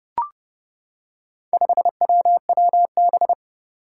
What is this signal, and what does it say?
Morse code, heard as a keyed tone, about 29 wpm, 700 Hz. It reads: 5WWB
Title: Four Challenging Letter-Number Combinations 30wpm